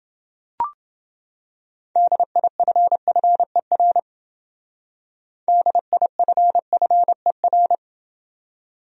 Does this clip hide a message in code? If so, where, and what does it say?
Yes: Morse code from 2.0 to 7.8 s, heard as a keyed tone, about 29 wpm, 700 Hz. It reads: DIFFER DIFFER